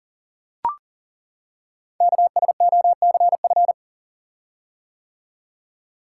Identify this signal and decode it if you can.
Morse code, heard as a keyed tone, about 40 wpm, 690 Hz. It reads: KSOCF
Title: Code Groups - 5 Letter:Numbers - 40wpm